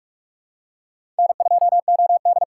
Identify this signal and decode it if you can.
Morse code, heard as a keyed tone, about 46 wpm, 700 Hz. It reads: N2OD
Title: USA Call Signs - 4 Characters - 6x Character Spacing - 45wpm